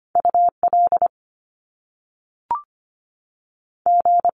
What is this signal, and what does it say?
Morse code, heard as a keyed tone, about 25 wpm, 700 Hz. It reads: UL G
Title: Top Two Letter Combinations in Words 25wpm